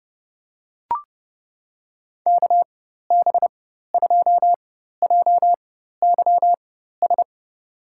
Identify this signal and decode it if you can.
Morse code, heard as a keyed tone, about 32 wpm, spaced as for 14 wpm, 700 Hz. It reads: KB2JYS
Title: USA Call Signs - 4x Character Spacing - 30wpm